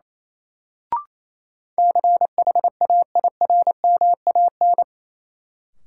Morse code, heard as a keyed tone, about 28 wpm, 700 Hz. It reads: CHAIRMAN
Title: Sets of 3 Words - New Word Drill - 28wpm